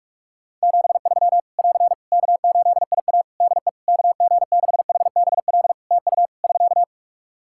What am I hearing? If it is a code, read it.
Morse code, heard as a keyed tone, about 46 wpm, 700 Hz. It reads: 73 <AR> K8IA DE KG6HBL TU <SK>
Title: Practice QSO - Exchange 4 - Ending - 45wpm